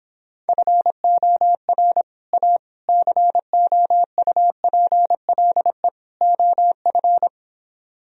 Morse code, heard as a keyed tone, about 26 wpm, 700 Hz. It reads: FOR A COUPLE OF